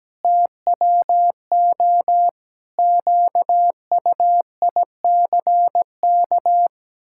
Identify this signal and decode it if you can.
Morse code, heard as a keyed tone, about 17 wpm, 700 Hz. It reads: TWO QUICK
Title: Pangram Challenge v2 - 17wpm